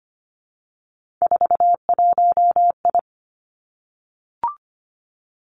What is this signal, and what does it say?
Morse code, heard as a keyed tone, about 25 wpm, 700 Hz. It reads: 41I
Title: Three Challenging Letter-Number Combinations 25wpm